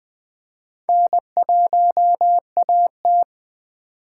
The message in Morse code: N1AT

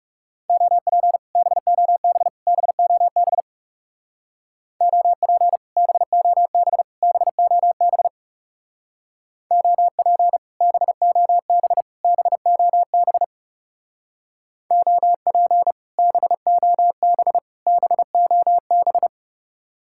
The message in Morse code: OP BOB BOB OP BOB BOB OP BOB BOB OP BOB BOB

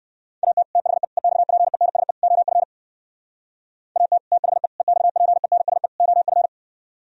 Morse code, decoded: AT THE EXPENSE OF AT THE EXPENSE OF

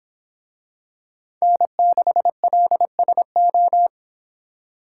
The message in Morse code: N6LSO

